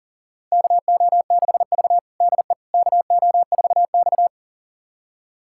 Morse code, heard as a keyed tone, about 40 wpm, 700 Hz. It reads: KO6V DE KO4X
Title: USA Callsign DE Callsign - 4 Characters - 40wpm